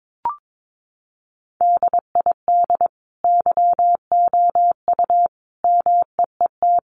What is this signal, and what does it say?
Morse code, heard as a keyed tone, about 22 wpm, 700 Hz. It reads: DID YOU MEET
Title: Sentences from Top 1100 Words 22wpm